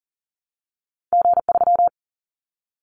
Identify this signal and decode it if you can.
Morse code, heard as a keyed tone, about 40 wpm, 700 Hz. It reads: G3
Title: Callsign Prefixes for Top 25 DXCC - Repeated 3x - 40wpm